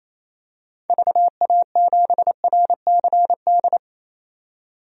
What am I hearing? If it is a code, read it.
Morse code, heard as a keyed tone, about 28 wpm, 700 Hz. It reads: VA7RCD
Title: DXCC-100 Call Signs - 6 Characters - 28wpm